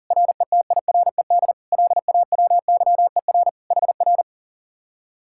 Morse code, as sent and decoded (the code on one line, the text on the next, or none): .-. - .. .-. -.. / .-.. .- .-- -.-- . .-. / .... .-.
RETIRED LAWYER HR